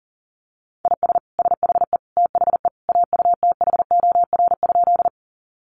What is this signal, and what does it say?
Morse code, heard as a keyed tone, about 40 wpm, 700 Hz. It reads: IS SHE THE AUTHOR?
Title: Sentences from Top 1200 Words 40wpm